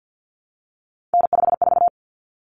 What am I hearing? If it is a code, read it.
Morse code, heard as a keyed tone, about 50 wpm, 700 Hz. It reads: N54
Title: Common Cut Numbers - 3-Digit 50wpm